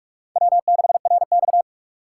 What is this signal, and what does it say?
Morse code, heard as a keyed tone, about 46 wpm, 700 Hz. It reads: W6RX
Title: USA Call Signs - 4 Characters - 4x Character Spacing - 45wpm